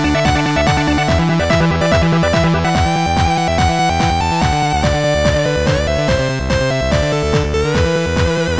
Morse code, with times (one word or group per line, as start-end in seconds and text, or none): none